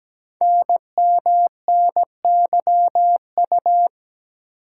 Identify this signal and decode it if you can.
Morse code, heard as a keyed tone, about 17 wpm, 700 Hz. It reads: NMNYU
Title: Code Groups - 5 Letters - 17wpm